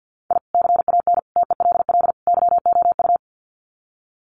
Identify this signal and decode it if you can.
Morse code, heard as a keyed tone, about 50 wpm, 700 Hz. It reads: I CAN TELL YOU
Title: Sets of 4 Words 50wpm